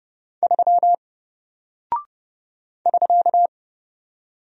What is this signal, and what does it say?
Morse code, heard as a keyed tone, about 30 wpm, 700 Hz. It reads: 3 <SK>
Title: Single Character (Letter:Number:Punctuation:Prosign) Alt 30wpm